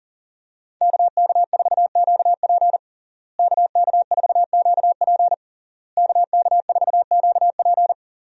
Morse code, heard as a keyed tone, about 40 wpm, 690 Hz. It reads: KK4QP KK4QP KK4QP
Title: USA Call Signs - 5 Characters - Repeated 3x - 40wpm